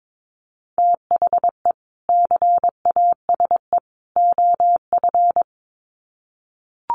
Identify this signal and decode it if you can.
Morse code, heard as a keyed tone, about 22 wpm, 700 Hz. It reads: THE CASE OF
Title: Sets of 3 Words 22wpm